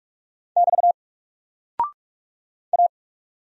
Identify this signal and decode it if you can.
Morse code, heard as a keyed tone, about 44 wpm, 700 Hz. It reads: <BT> A
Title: Single Character (Letter:Number:Punctuation:Prosign) Alt 45wpm